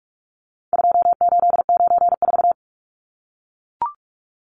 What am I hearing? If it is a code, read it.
Morse code, heard as a keyed tone, about 44 wpm, 700 Hz. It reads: 2894
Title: Numbers - 4-Digit 45wpm